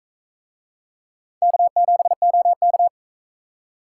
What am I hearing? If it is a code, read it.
Morse code, heard as a keyed tone, about 42 wpm, 700 Hz. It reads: K7OK